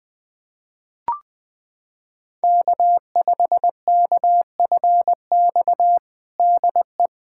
Morse code, heard as a keyed tone, about 21 wpm, 700 Hz. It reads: K5KFX DE
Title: USA Callsign DE Callsign - 5 Characters - 20wpm